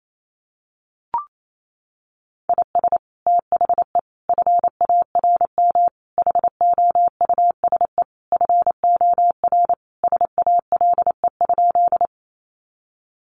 Morse code, heard as a keyed tone, about 28 wpm, 700 Hz. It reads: IS THE FARM HOUSE FOR SALE?
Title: Sentences from Top 1400 Words 28wpm